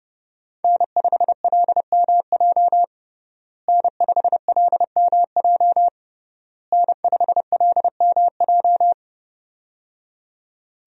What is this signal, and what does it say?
Morse code, heard as a keyed tone, about 30 wpm, 700 Hz. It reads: N5LMJ N5LMJ N5LMJ
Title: USA Call Signs - 5 Characters - Repeated 3x - 30wpm